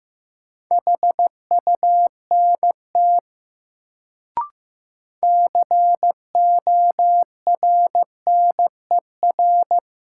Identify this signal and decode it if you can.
Morse code, heard as a keyed tone, about 15 wpm, 700 Hz. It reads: HUNT CORNER
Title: Sentences from Top 1600 Words - New Word Drill - 15wpm